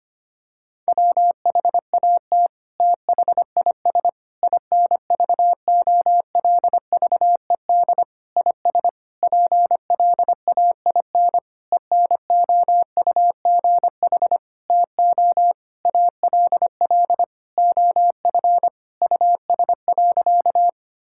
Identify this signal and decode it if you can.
Morse code, heard as a keyed tone, about 25 wpm, 700 Hz. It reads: WHAT THIS INVOLVED IS PLAIN ENOUGH TO ALL OF US.
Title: Sentences from Top 1000 Words - Review Part 5 - 25wpm